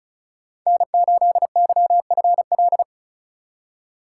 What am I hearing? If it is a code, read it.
Morse code, heard as a keyed tone, about 34 wpm, 700 Hz. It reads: N8YFL
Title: USA Call Signs - 5 Characters - 2x Character Spacing - 35wpm